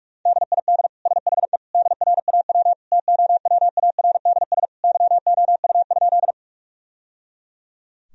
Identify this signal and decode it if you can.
Morse code, heard as a keyed tone, about 43 wpm, 700 Hz. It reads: DID SHE DRAW TOWARDS YOU?
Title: Sentences from Top 1000 Words - Review Part 2 - 45wpm